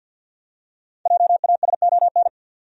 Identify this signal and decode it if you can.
Morse code, heard as a keyed tone, about 48 wpm, 700 Hz. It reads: JASON
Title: USA Common Names - 50wpm